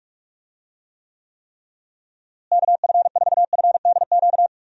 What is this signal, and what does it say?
Morse code, heard as a keyed tone, about 46 wpm, 700 Hz. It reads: KF4FDQ
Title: USA Call Signs - 6 Characters - Repeated 3x - 45wpm